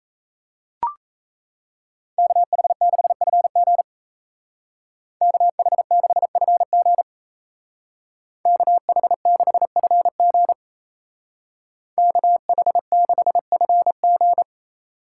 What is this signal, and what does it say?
Morse code, heard as a keyed tone, about 36 wpm, 700 Hz. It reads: KH6FG KH6FG KH6FG KH6FG